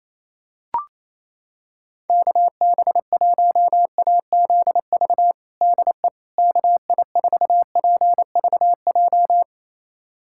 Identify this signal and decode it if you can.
Morse code, heard as a keyed tone, about 28 wpm, 700 Hz. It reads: KB1AZV DE KI4PVJ
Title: USA Callsign DE Callsign - 6 Characters - 28wpm